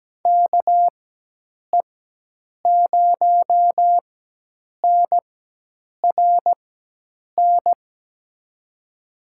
Morse code, heard as a keyed tone, about 17 wpm, spaced as for 8 wpm, 700 Hz. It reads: KE0NRN
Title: USA Call Signs - 4x Character Spacing - 17wpm